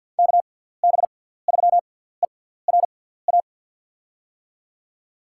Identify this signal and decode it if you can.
Morse code, heard as a keyed tone, about 53 wpm, spaced as for 17 wpm, 700 Hz. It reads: KB3ERA
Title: USA Call Signs - 6 Characters - 6x Character Spacing - 50wpm